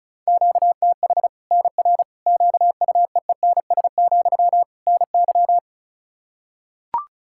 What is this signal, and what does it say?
Morse code, heard as a keyed tone, about 34 wpm, 700 Hz. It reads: QTH NR QUEENS, NY
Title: Practice QSO - Exchange 1 - Beginning - 35wpm